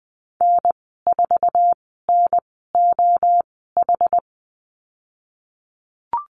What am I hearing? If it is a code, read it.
Morse code, heard as a keyed tone, about 20 wpm, spaced as for 15 wpm, 700 Hz. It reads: N4NOH
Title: 062 Lesson 11g - Letter h - Call Signs 20wpm